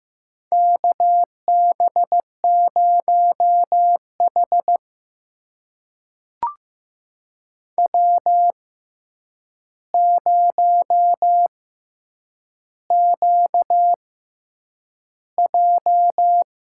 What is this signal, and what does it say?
Morse code, heard as a keyed tone, about 15 wpm, spaced as for 8 wpm, 700 Hz. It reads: KB0H W0QJ
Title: USA Call Signs - 4 Characters - 6x Character Spacing - 15wpm